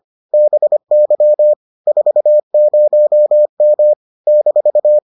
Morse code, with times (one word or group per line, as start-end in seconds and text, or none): 0.3-1.5 BY
1.9-3.9 40M
4.3-5.0 -